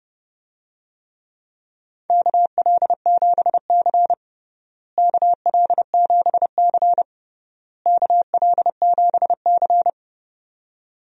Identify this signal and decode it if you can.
Morse code, heard as a keyed tone, about 31 wpm, 700 Hz. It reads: KL7C KL7C KL7C
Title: DXCC-100 Call Signs - 4 Characters - Repeated 3x - 30wpm